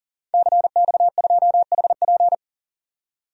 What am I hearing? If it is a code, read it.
Morse code, heard as a keyed tone, about 40 wpm, 700 Hz. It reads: CX2HP
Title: DXCC-100 Call Signs - 5 Characters - 40wpm